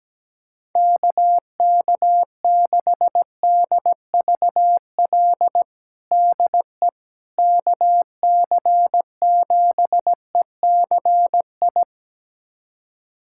Morse code, heard as a keyed tone, about 17 wpm, 700 Hz. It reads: KK6DVL DE KC7ECI
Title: USA Callsign DE Callsign - 6 Characters - 17wpm